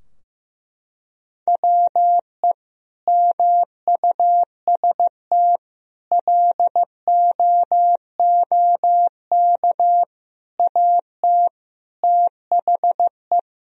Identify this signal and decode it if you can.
Morse code, heard as a keyed tone, about 15 wpm, 700 Hz. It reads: WE MUST LOOK AT THE
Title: Sentences from Top 1000 Words - Review Part 2 - 15wpm